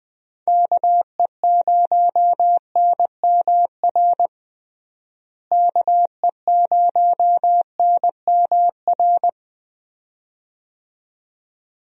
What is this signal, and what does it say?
Morse code, heard as a keyed tone, about 21 wpm, 700 Hz. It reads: KE0NMR KE0NMR
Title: USA Call Signs - 6 Characters - Repeated 2x - 20wpm